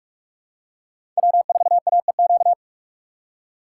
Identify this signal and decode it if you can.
Morse code, heard as a keyed tone, about 46 wpm, 700 Hz. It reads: W4AEQ
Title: USA Call Signs - 5 Characters - 4x Character Spacing - 45wpm